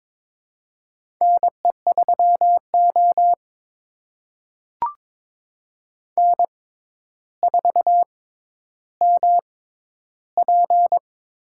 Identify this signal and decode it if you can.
Morse code, heard as a keyed tone, about 22 wpm, spaced as for 11 wpm, 700 Hz. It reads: NE3O N4MP
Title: USA Call Signs - 4 Characters - 6x Character Spacing - 22wpm